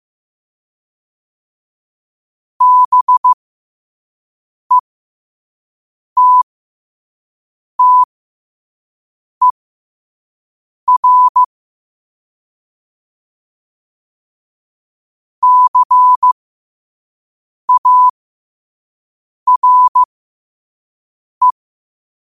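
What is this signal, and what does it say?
Morse code, heard as a keyed tone, about 15 wpm, spaced as for 5 wpm, 1000 Hz. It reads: BETTER CARE
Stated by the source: Morse code practice at 5 wpm.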